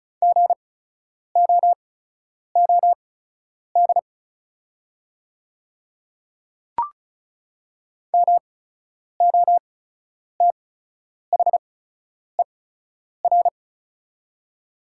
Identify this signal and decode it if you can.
Morse code, heard as a keyed tone, about 34 wpm, spaced as for 10 wpm, 700 Hz. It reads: GOOD MOTHER
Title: 8x Character Spacing - Top 300 Words - 35wpm